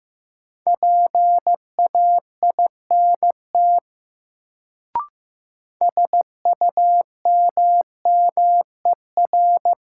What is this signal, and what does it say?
Morse code, heard as a keyed tone, about 15 wpm, 700 Hz. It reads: PAINT SUMMER
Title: Sentences from Top 1000 Words - Review Part 6 - New Word Drill - 15wpm